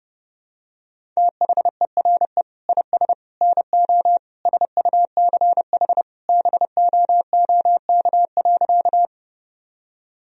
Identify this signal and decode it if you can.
Morse code, heard as a keyed tone, about 30 wpm, 700 Hz. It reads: THERE IS NO SUCH BOOK.
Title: Sentences from Top 1000 Words - Review Part 1 - 30wpm